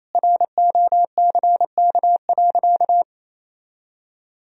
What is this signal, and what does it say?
Morse code, heard as a keyed tone, about 28 wpm, 700 Hz. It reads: ROCK.
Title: Sentences from Top 1000 Words - Review Part 2 - 28wpm